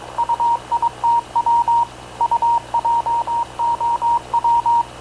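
Morse code, heard as a keyed tone, about 22 wpm, 940 Hz. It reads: UITW UJOW
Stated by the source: Havana Radio / CLA sending Spanish press (PX) in Morse, recorded by Radio Officer Paul Dumesnil onboard CNAV QUEST/CZDO.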